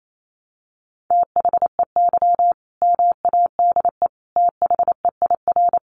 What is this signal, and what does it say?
Morse code, heard as a keyed tone, about 28 wpm, 700 Hz. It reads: THEY MADE THEIR
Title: Sentences from Top 1100 Words 28wpm